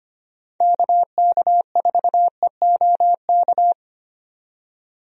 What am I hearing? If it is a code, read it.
Morse code, heard as a keyed tone, about 25 wpm, 700 Hz. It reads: KK4EOK
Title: USA Call Signs - 6 Characters - 25wpm